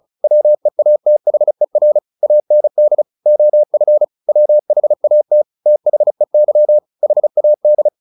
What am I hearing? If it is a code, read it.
Morse code, heard as a keyed tone, about 34 wpm, 600 Hz. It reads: WEATHER AND OF WHAT THEY HAD
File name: the old man and the sea / the old man and the sea 1 35wpm